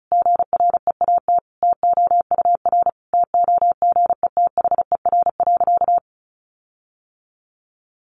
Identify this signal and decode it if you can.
Morse code, heard as a keyed tone, about 34 wpm, 700 Hz. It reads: GREAT TOUR TOGETHER.
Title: Sentences from Top 1400 Words 35wpm